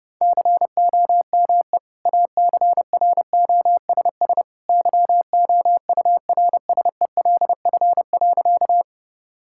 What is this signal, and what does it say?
Morse code, heard as a keyed tone, about 32 wpm, 700 Hz. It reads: COME ACROSS YOURSELF.
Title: Sentences from Top 1000 Words - Review Part 5 - 30wpm